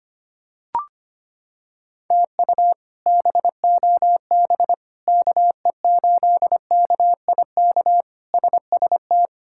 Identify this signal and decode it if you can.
Morse code, heard as a keyed tone, about 25 wpm, 700 Hz. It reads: TU BOB KE8KIK SST
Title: K1USN SST QSO Practice - 25wpm